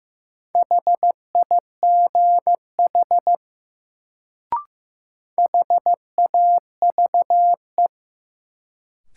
Morse code, heard as a keyed tone, about 15 wpm, 700 Hz. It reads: HIGH HAVE
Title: Sets of 2 Words - New Word Drill - 15wpm